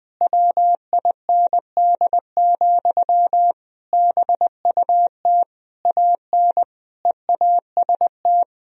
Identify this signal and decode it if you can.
Morse code, heard as a keyed tone, about 20 wpm, 700 Hz. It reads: WIND, BUT AN EAST
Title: Sentences from Top 1000 Words - Review Part 3 - 20wpm